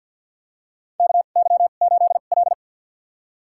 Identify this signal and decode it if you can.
Morse code, heard as a keyed tone, about 51 wpm, spaced as for 36 wpm, 700 Hz. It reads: KY8L
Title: USA Call Signs - 4 Characters - 2x Character Spacing - 50wpm